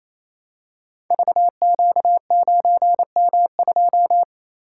Morse code, heard as a keyed tone, about 28 wpm, 700 Hz. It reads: VQ9M2
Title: Code Groups - 5 Letter:Numbers - 28wpm